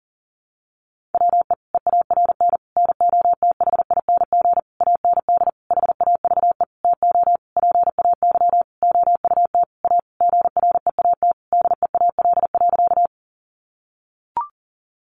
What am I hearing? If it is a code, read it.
Morse code, heard as a keyed tone, about 40 wpm, 700 Hz. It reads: WE EARN NOTHING AND HAVE TO PAY OUT A GREAT DEAL.
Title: Sentences from Top 1500 Words 40wpm